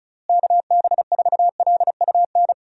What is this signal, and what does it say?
Morse code, heard as a keyed tone, about 36 wpm, 700 Hz. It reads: KB4LUN
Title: USA Call Signs - 6 Characters - Repeated 2x - 35wpm